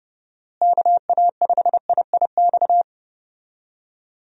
Morse code, heard as a keyed tone, about 30 wpm, 700 Hz. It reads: KA5IIX
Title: USA Call Signs - 6 Characters - Repeated 3x - 30wpm